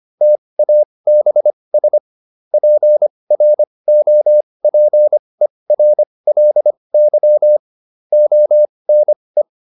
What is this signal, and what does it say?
Morse code, heard as a keyed tone, about 25 wpm, spaced as for 20 wpm, 600 Hz. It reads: TABS PROPERLY ONE
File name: SS215 20